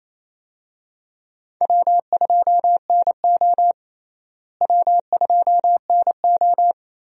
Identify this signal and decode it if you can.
Morse code, heard as a keyed tone, about 28 wpm, 700 Hz. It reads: W2NO W2NO
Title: USA Call Signs - 4 Characters - Repeated 2x - 28wpm